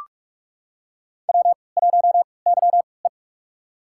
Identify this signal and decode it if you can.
Morse code, heard as a keyed tone, about 41 wpm, spaced as for 25 wpm, 700 Hz. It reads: W1YE
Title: USA Call Signs - 4 Characters - 3x Character Spacing - 45wpm